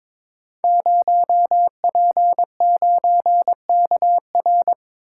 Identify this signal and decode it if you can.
Morse code, heard as a keyed tone, about 22 wpm, 700 Hz. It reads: 0P9KR
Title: Code Groups - 5 Letter:Numbers - 22wpm